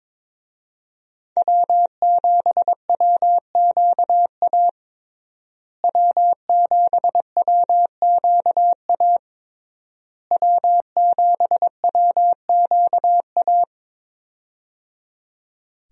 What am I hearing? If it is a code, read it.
Morse code, heard as a keyed tone, about 22 wpm, 700 Hz. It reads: W7WQA W7WQA W7WQA
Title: USA Call Signs - 5 Characters - Repeated 3x - 22wpm